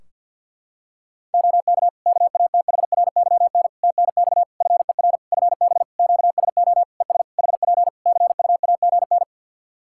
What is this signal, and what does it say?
Morse code, heard as a keyed tone, about 52 wpm, 700 Hz. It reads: OK KATHRYN TNX FER FB QSO ES HP CUAGN
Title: K1USN SST QSO Practice 50wpm